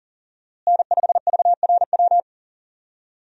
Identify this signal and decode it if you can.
Morse code, heard as a keyed tone, about 38 wpm, 700 Hz. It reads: N5VRW
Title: USA Call Signs - 5 Characters - Repeated 3x - 40wpm